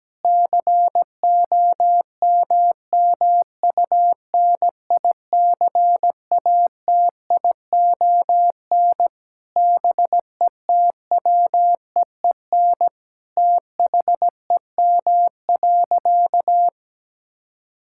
Morse code, heard as a keyed tone, about 17 wpm, 700 Hz. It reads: COMMUNICATION BETWEEN THEM.
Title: Sentences from Top 1000 Words - Review Part 6 - 17wpm